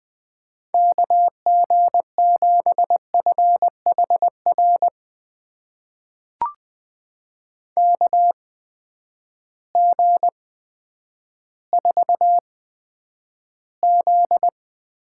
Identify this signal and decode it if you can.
Morse code, heard as a keyed tone, about 20 wpm, spaced as for 10 wpm, 700 Hz. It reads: KG7FHR KG4Z
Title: USA Call Signs - 6 Characters - 8x Character Spacing - 20wpm